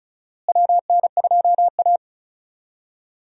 Morse code, heard as a keyed tone, about 36 wpm, 700 Hz. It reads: WN2A
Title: USA Call Signs - 4 Characters - Repeated 2x - 35wpm